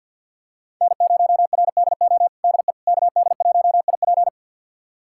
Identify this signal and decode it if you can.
Morse code, heard as a keyed tone, about 46 wpm, 690 Hz. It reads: N0RDO DE KD1IP
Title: USA Callsign DE Callsign - 5 Characters - 50wpm